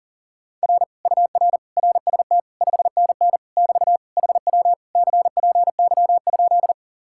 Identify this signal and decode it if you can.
Morse code, heard as a keyed tone, about 47 wpm, 700 Hz. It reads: R UR RST 5NN <BT> HW CPY?
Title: Speed-Racing Common QSO Phrases - 40wpm